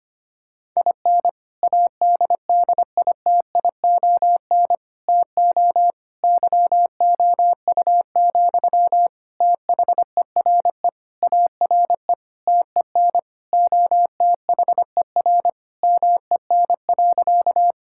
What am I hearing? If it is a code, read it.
Morse code, heard as a keyed tone, about 25 wpm, 700 Hz. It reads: IN ADDITION TO YOU, THERE ARE TEN OTHER MEN.
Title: Sentences from Top 1000 Words - Review Part 6 - 25wpm